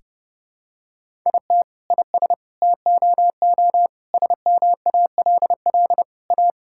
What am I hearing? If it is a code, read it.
Morse code, heard as a keyed tone, about 32 wpm, 700 Hz. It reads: IT IS TOO SMALL A
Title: Sentences from Top 1000 Words - Review Part 6 - 30wpm